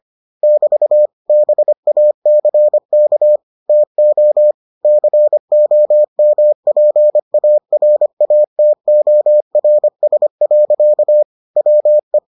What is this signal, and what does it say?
Morse code, heard as a keyed tone, about 25 wpm, 600 Hz. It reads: <BT> BACK TO COMPARATORS. WE